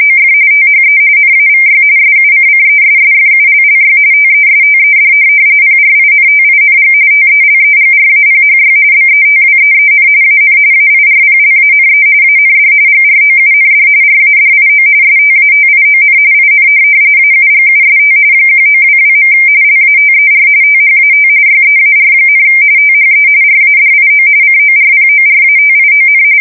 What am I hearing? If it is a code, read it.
RTTY, about 45.45 baud. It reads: OF THE NAVY SUCCESSFULLY TESTED PRINTING TELEGRAPHY BETWEEN AN AIRPLANE AND GROUND RADIO STATION IN AUGUST 1922. LATER THAT YEAR, THE RADIO CORPORATION
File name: fsk170 / teletype-RTTY